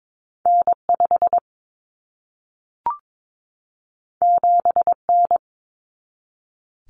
Morse code, heard as a keyed tone, about 22 wpm, 700 Hz. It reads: N5 7N
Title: Common Cut Numbers - 2-Digit 22wpm